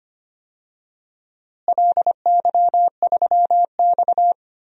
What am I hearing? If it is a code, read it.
Morse code, heard as a keyed tone, about 25 wpm, 700 Hz. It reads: LY3X
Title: DXCC-100 Call Signs - 4 Characters - Repeated 3x - 25wpm